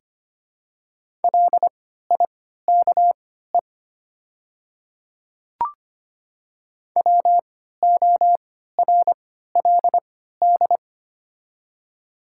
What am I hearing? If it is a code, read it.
Morse code, heard as a keyed tone, about 25 wpm, spaced as for 14 wpm, 700 Hz. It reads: LIKE WORLD
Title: 3x Character Spacing - Top 300 Words - 25wpm